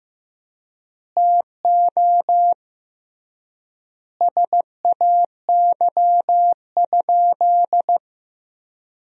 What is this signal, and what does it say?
Morse code, heard as a keyed tone, about 15 wpm, 700 Hz. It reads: TO SAY?
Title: 3x Word-Spacing - Sentences from Top 200 Words - 15wpm